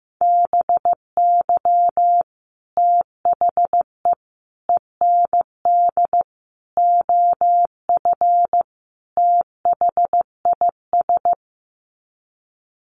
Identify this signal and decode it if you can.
Morse code, heard as a keyed tone, about 15 wpm, 700 Hz. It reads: BY THE END OF THIS
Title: Sets of 5 Words Encore 15wpm